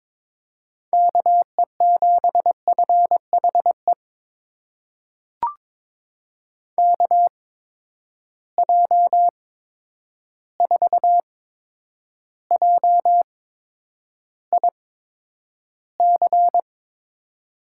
Morse code, heard as a keyed tone, about 22 wpm, spaced as for 9 wpm, 700 Hz. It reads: KE7FHE KJ4JIC
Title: USA Call Signs - 6 Characters - 8x Character Spacing - 22wpm